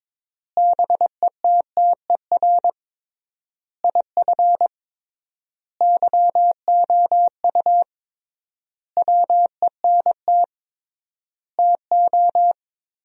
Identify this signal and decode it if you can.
Morse code, heard as a keyed tone, about 22 wpm, 700 Hz. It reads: BETTER IF YOU WENT TO
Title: 3x Word-Spacing - Sentences from Top 400 Words - 22wpm